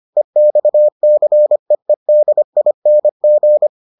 Morse code, heard as a keyed tone, about 25 wpm, 600 Hz. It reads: EXCEEDING